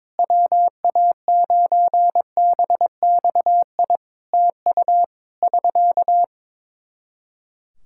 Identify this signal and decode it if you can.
Morse code, heard as a keyed tone, about 22 wpm, 700 Hz. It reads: WA9BXI TU <SK>
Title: K1USN SST QSO Practice 22wpm